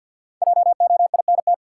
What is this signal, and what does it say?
Morse code, heard as a keyed tone, about 53 wpm, 700 Hz. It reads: JOINT
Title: Sentences from Top 1300 Words - New Word Drill - 50wpm